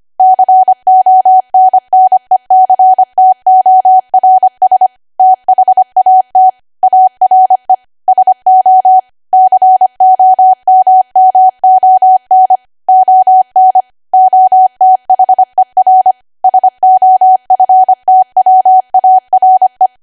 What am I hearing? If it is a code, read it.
Morse code, heard as a keyed tone, about 25 wpm, 750 Hz. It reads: CONNECTORS THAT ARE SO COMMON ON OTHER SOFTWARE